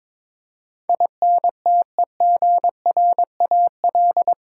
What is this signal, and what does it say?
Morse code, heard as a keyed tone, about 22 wpm, 700 Hz. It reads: INTEGRAL
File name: practice / 22wpm / Eight Letter Words - 22wpm